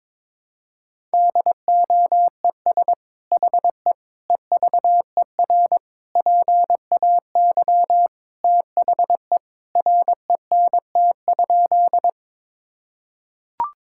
Morse code, heard as a keyed tone, about 21 wpm, 700 Hz. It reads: DOES HE EVER PAY THE RENT?
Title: Sentences from Top 1000 Words - Review Part 5 - 22wpm